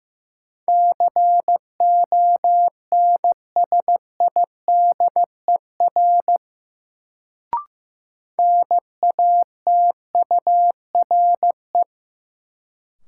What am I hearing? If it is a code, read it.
Morse code, heard as a keyed tone, about 15 wpm, 700 Hz. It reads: CONSIDER NATURE
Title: Sentences from Top 600 Words - New Word Drill - 15wpm